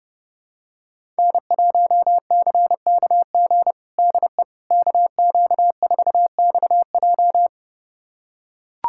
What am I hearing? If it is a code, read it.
Morse code, heard as a keyed tone, about 30 wpm, 700 Hz. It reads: N1CKG DE KQ4XJ
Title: USA Callsign DE Callsign - 5 Characters - 30wpm